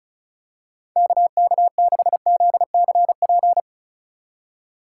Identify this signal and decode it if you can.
Morse code, heard as a keyed tone, about 34 wpm, 700 Hz. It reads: KK6ZCP
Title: USA Call Signs - 6 Characters - 2x Character Spacing - 35wpm